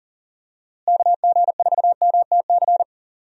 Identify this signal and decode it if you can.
Morse code, heard as a keyed tone, about 39 wpm, 700 Hz. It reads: KG4MTC
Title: USA Call Signs - 6 Characters - Repeated 3x - 40wpm